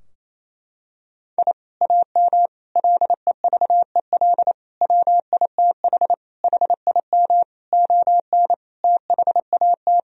Morse code, heard as a keyed tone, about 28 wpm, 700 Hz. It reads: I AM LEVEL WITH HIM ON THAT
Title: Sentences from Top 1000 Words - Review Part 4 - 28wpm